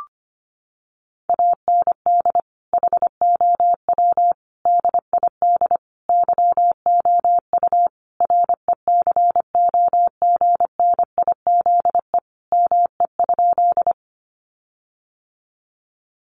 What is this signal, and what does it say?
Morse code, heard as a keyed tone, about 25 wpm, 700 Hz. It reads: AND HOW DID YOU RECOGNIZE ME?
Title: Sentences from Top 1400 Words 25wpm